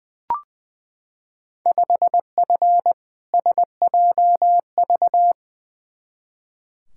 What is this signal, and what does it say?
Morse code, heard as a keyed tone, about 20 wpm, 700 Hz. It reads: 5F SJV
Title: Field Day Exchanges Encore - 20wpm